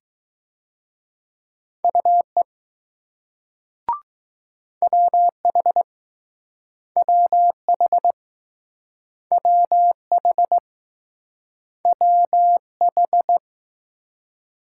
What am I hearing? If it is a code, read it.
Morse code, heard as a keyed tone, about 21 wpm, 700 Hz. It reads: UE WH WH WH WH